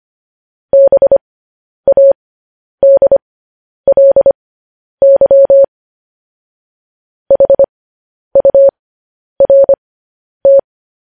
Morse code, heard as a keyed tone, about 25 wpm, spaced as for 10 wpm, 550 Hz. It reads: BADLY HURT
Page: News Headlines in Morse Code at 10 WPM